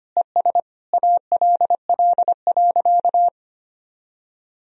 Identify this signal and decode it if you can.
Morse code, heard as a keyed tone, about 25 wpm, 700 Hz. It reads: ES ALL.
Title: Sentences from Top 200 Words and Abbreviations 25wpm